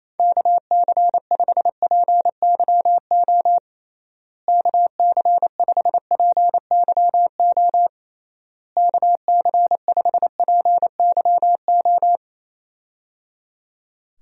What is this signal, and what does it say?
Morse code, heard as a keyed tone, about 27 wpm, 700 Hz. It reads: KC5PYO KC5PYO KC5PYO
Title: USA Call Signs - 6 Characters - Repeated 3x - 28wpm